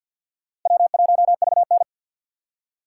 Morse code, heard as a keyed tone, about 50 wpm, 700 Hz. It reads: W1VN
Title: USA Call Signs - 4 Characters - 3x Character Spacing - 50wpm